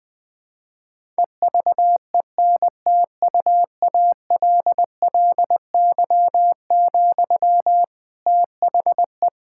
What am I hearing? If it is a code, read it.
Morse code, heard as a keyed tone, about 20 wpm, 700 Hz. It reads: EVENTUALLY, THE
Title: Sentences from Top 1000 Words - Review Part 6 - 20wpm